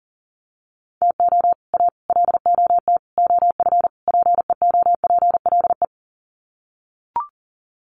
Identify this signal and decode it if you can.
Morse code, heard as a keyed tone, about 40 wpm, 700 Hz. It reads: TO A LOT OF PEOPLE
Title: Sets of 5 Words Encore 40wpm